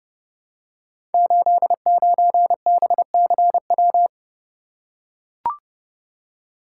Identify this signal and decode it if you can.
Morse code, heard as a keyed tone, about 30 wpm, 700 Hz. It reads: 89BCW
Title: Code Groups - 5 Letter:Numbers - 30wpm